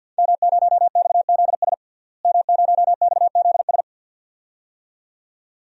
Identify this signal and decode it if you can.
Morse code, heard as a keyed tone, about 51 wpm, 700 Hz. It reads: M0XZS M0XZS
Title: DXCC-100 Call Signs - 5 Characters - Repeated 2x - 50wpm